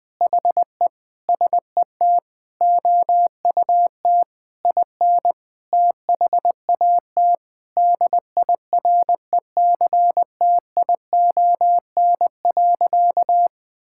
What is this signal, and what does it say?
Morse code, heard as a keyed tone, about 20 wpm, 700 Hz. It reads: HE SET OUT IN THAT DIRECTION.
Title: Sentences from Top 1000 Words - Review Part 5 - 20wpm